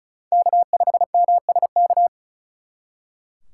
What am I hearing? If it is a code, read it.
Morse code, heard as a keyed tone, about 43 wpm, 700 Hz. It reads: K5MSK
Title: Speed Racing - USA Call Signs 35wpm